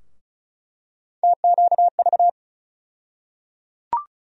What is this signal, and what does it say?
Morse code, heard as a keyed tone, about 36 wpm, 700 Hz. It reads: TQV
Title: USA Call Sign Postfixes - 3 Letters 35wpm